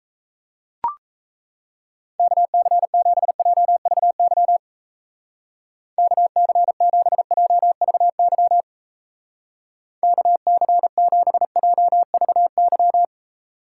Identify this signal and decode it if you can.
Morse code, heard as a keyed tone, about 35 wpm, 700 Hz. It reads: KC7JVY KC7JVY KC7JVY
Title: Speed Racing - USA 2x3 Call Signs 28wpm